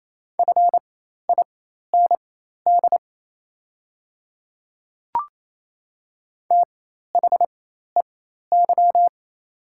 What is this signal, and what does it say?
Morse code, heard as a keyed tone, about 28 wpm, spaced as for 13 wpm, 700 Hz. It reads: FIND THEY
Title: 4x Character Spacing - Top 300 Words - 28wpm